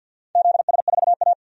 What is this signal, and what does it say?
Morse code, heard as a keyed tone, about 46 wpm, 700 Hz. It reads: ZS4A
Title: DXCC-100 Call Signs - 4 Characters - 50wpm